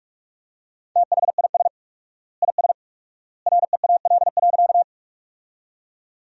Morse code, heard as a keyed tone, about 42 wpm, 700 Hz. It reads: THIS IS REAL.
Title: 4x Word-Spacing - Sentences from Top 300 Words - 45wpm